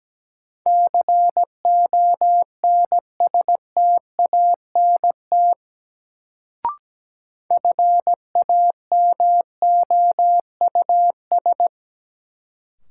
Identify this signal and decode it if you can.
Morse code, heard as a keyed tone, about 17 wpm, 700 Hz. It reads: CONSTANT FAMOUS
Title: Sentences from Top 1500 Words - New Word Drill Alt - 17wpm